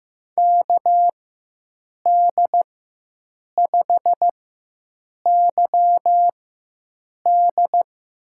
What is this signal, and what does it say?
Morse code, heard as a keyed tone, about 15 wpm, spaced as for 7 wpm, 700 Hz. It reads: KD5YD
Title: USA Call Signs - 6 Characters - 4x Character Spacing - 15wpm